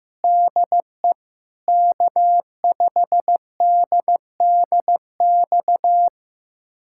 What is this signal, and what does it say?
Morse code, heard as a keyed tone, about 15 wpm, 700 Hz. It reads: DE K5DDX
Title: USA Callsign DE Callsign - 5 Characters - 15wpm